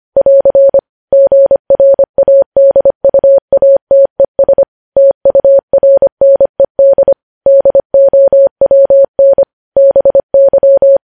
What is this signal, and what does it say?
Morse code, heard as a keyed tone, about 25 wpm, 550 Hz. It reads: <AR> GRADUATES TURNED DOWN BY
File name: podcast / News headlines-1x-25x25